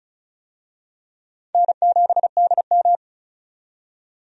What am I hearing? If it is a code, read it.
Morse code, heard as a keyed tone, about 36 wpm, 700 Hz. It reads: N7DM
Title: USA Call Signs - 4 Characters - 35wpm